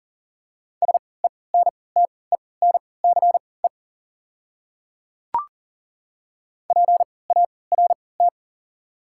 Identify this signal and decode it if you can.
Morse code, heard as a keyed tone, about 40 wpm, spaced as for 23 wpm, 690 Hz. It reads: SENTENCE PART
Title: 3x Character Spacing - Top 200 Words - 40wpm